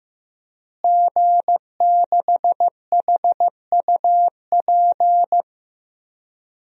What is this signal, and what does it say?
Morse code, heard as a keyed tone, about 15 wpm, 700 Hz. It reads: G6HUP
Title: DXCC-100 Call Signs - 5 Characters - Repeated 2x - 15wpm